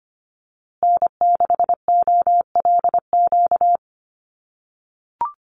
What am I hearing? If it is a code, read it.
Morse code, heard as a keyed tone, about 25 wpm, 700 Hz. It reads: N6OLQ
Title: USA Call Signs Repeated 3x 25wpm